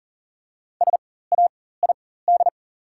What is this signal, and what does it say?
Morse code, heard as a keyed tone, about 37 wpm, spaced as for 18 wpm, 700 Hz. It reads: SAID
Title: 4x Character Spacing - Top 300 Words - 40wpm